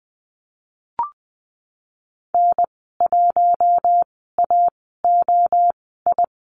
Morse code, heard as a keyed tone, about 20 wpm, spaced as for 15 wpm, 700 Hz. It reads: N1AOI
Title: 048 Lesson 9g - Number 4 - Call Signs 20wpm